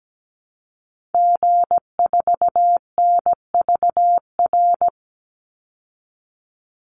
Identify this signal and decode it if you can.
Morse code, heard as a keyed tone, about 17 wpm, 700 Hz. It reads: G4NVR
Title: Call Signs Repeated 3x - 17wpm